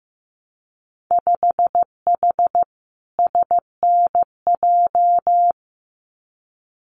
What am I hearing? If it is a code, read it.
Morse code, heard as a keyed tone, about 15 wpm, 700 Hz. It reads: 5H SNJ
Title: Winter Field Day Exchanges Encore 15wpm